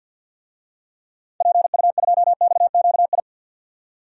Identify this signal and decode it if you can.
Morse code, heard as a keyed tone, about 47 wpm, 700 Hz. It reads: PU2XQI
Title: DXCC-100 Call Signs - 6 Characters - 50wpm